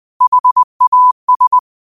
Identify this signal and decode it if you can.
Morse code, heard as a keyed tone, about 20 wpm, 1000 Hz. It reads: HAS
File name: qotdfiles / 20wpm